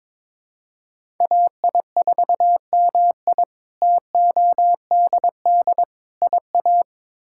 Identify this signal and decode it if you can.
Morse code, heard as a keyed tone, about 22 wpm, 700 Hz. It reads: AI4MI TODD IA
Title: K1USN SST QSO Practice - 22wpm